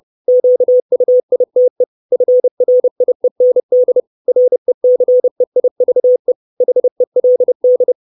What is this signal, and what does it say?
Morse code, heard as a keyed tone, about 30 wpm, 500 Hz. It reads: QUITE FRIEND RECEIVE HELD
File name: 632 Words 30WPM